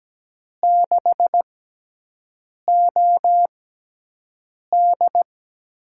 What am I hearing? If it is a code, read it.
Morse code, heard as a keyed tone, about 17 wpm, spaced as for 6 wpm, 700 Hz. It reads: 6OD
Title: USA Call Signs - 4 Characters - 6x Character Spacing - 17wpm